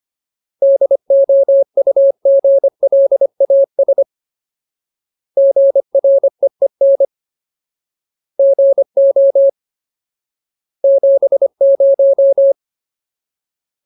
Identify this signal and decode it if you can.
Morse code, heard as a keyed tone, about 25 wpm, 550 Hz. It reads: DOUGLAS GREEN GO 70
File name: Ultimate Word List / 25WPM Ultimate